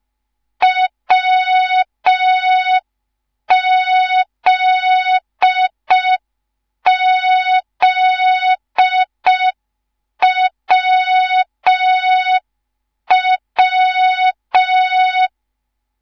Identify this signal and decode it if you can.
Morse code, heard as a keyed tone, about 5 wpm, 3700 Hz. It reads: WZZWW